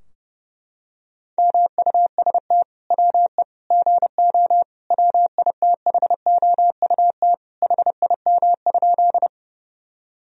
Morse code, heard as a keyed tone, about 30 wpm, 700 Hz. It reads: MUST WE GO WITHOUT HIM?
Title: Sentences from Top 1000 Words - Review Part 3 - 30wpm